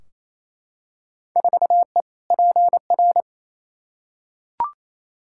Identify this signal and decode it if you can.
Morse code, heard as a keyed tone, about 28 wpm, 700 Hz. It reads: 4E PR
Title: Field Day Exchanges Encore - 28wpm